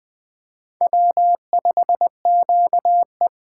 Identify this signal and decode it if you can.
Morse code, heard as a keyed tone, about 20 wpm, 700 Hz. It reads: W5QE